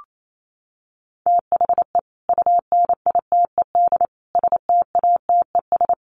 Morse code, heard as a keyed tone, about 28 wpm, 700 Hz. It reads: THE UNITED STATES
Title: Sets of 4 Words 28wpm